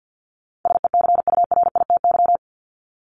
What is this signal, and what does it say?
Morse code, heard as a keyed tone, about 50 wpm, 700 Hz. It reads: SECURITY
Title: Top 501-750 Words 50wpm